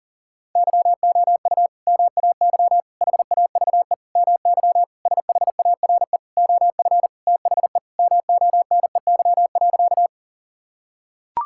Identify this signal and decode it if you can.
Morse code, heard as a keyed tone, about 40 wpm, 700 Hz. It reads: YOU MAY HAVE MY SHARE OF THE MONEY.
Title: Sentences from Top 1000 Words - Review Part 6 - 40wpm